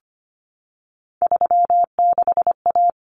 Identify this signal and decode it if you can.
Morse code, heard as a keyed tone, about 25 wpm, 700 Hz. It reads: 36A
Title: Common Cut Numbers - 3-Digit 25wpm